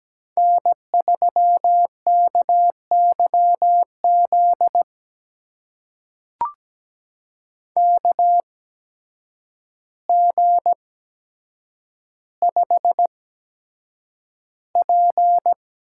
Morse code, heard as a keyed tone, about 17 wpm, spaced as for 8 wpm, 700 Hz. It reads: N3KYZ KG5P
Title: USA Call Signs - 8x Character Spacing - 17wpm